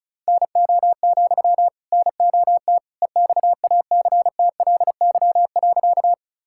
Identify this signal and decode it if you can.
Morse code, heard as a keyed tone, about 36 wpm, 700 Hz. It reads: NO, NOT EXACTLY.
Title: Sentences from Top 1000 Words - Review Part 6 - 35wpm